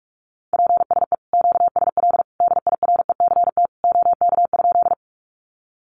Morse code, heard as a keyed tone, about 44 wpm, 700 Hz. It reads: PSE QSL DIRECT OK?
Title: Common QSO Phrases 45wpm